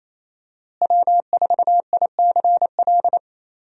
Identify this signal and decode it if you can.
Morse code, heard as a keyed tone, about 28 wpm, 700 Hz. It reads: W4ICL
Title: USA Call Signs - 5 Characters - Repeated 3x - 28wpm